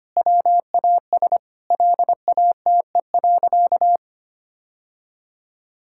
Morse code, heard as a keyed tone, about 25 wpm, 700 Hz. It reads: WAS LATE.
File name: practice / 25wpm / Sentences with 3-Word Phrases - 25wpm